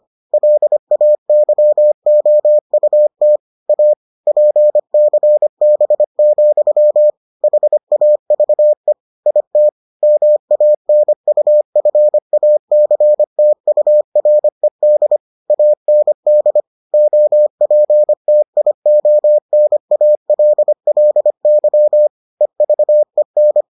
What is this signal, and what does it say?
Morse code, heard as a keyed tone, about 25 wpm, 600 Hz. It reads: LAYOUT A PCB, HAVE IT MANUFACTURED AND OPTIONALLY EVEN